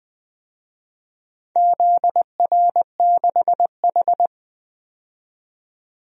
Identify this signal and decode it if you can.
Morse code, heard as a keyed tone, about 20 wpm, 700 Hz. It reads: ZR6H
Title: DXCC-100 Call Signs - 4 Characters - 20wpm